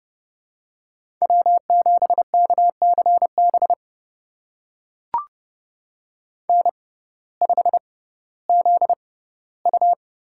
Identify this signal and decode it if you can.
Morse code, heard as a keyed tone, about 30 wpm, spaced as for 17 wpm, 700 Hz. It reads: W7KCB N5ZU
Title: USA Call Signs - 5 Characters - 6x Character Spacing - 30wpm